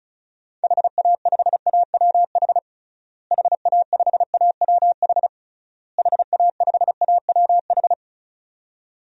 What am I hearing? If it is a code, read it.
Morse code, heard as a keyed tone, about 36 wpm, 700 Hz. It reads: HA5AWH HA5AWH HA5AWH
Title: DXCC-100 Call Signs - 6 Characters - Repeated 3x - 35wpm